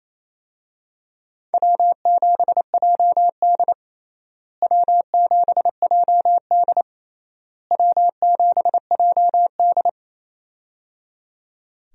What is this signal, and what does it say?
Morse code, heard as a keyed tone, about 28 wpm, 700 Hz. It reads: W7JD W7JD W7JD
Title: USA Call Signs - 4 Characters - Repeated 3x - 28wpm